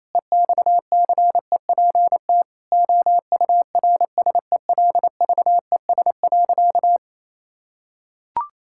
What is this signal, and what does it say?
Morse code, heard as a keyed tone, about 28 wpm, 700 Hz. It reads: EXCEPT OURSELVES.
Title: Sentences from Top 1600 Words 28wpm